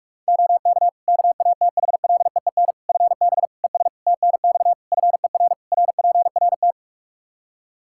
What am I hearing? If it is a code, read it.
Morse code, heard as a keyed tone, about 46 wpm, 700 Hz. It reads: OK KATHLEEN FB ES TNX FER RPRT
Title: Practice QSO - Exchange 2 - Info - 45wpm